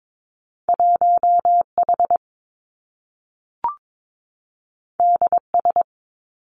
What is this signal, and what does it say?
Morse code, heard as a keyed tone, about 22 wpm, 700 Hz. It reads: 1 H D S
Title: Two Challenging Letter-Number Combinations 22wpm